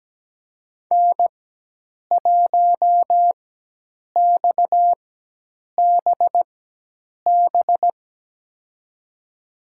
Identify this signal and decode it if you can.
Morse code, heard as a keyed tone, about 17 wpm, spaced as for 8 wpm, 700 Hz. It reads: N1XBB
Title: USA Call Signs - 4x Character Spacing - 17wpm